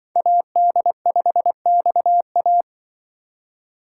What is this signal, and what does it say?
Morse code, heard as a keyed tone, about 24 wpm, 700 Hz. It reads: AD5XA
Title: Speed Racing - USA 2x2 Call Signs 20wpm